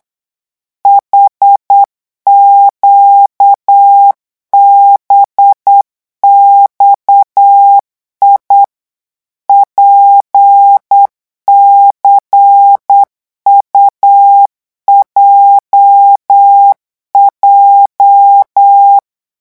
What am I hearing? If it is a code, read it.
Morse code, heard as a keyed tone, about 8 wpm, 800 Hz. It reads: HQBXI PCUJJ
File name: morse mp3 / revision8-vitesse 8 mots